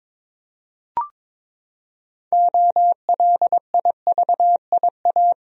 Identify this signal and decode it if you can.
Morse code, heard as a keyed tone, about 22 wpm, 700 Hz. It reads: OLIVIA
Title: USA Common Names - 22wpm